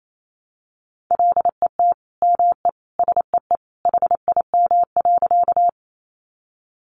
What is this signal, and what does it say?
Morse code, heard as a keyed tone, about 28 wpm, 700 Hz. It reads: LET ME SEE HIM.
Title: Sentences from Top 300 Words 28wpm